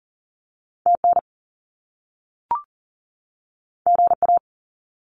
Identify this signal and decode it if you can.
Morse code, heard as a keyed tone, about 38 wpm, 700 Hz. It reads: TN GA
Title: US States 40wpm